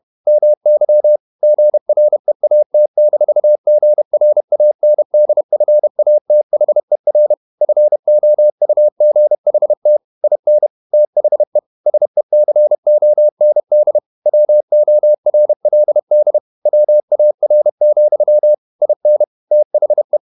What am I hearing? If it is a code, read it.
Morse code, heard as a keyed tone, about 31 wpm, 600 Hz. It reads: MY GREAT-GRANDFATHER FOUGHT IN THE SECOND WORLD WAR, IN THE
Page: Never 31wpm | CW med Gnister